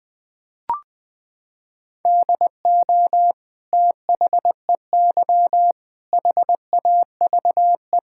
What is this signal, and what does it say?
Morse code, heard as a keyed tone, about 21 wpm, 700 Hz. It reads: DO THEY HAVE
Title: Sentences from Top 1600 Words 20wpm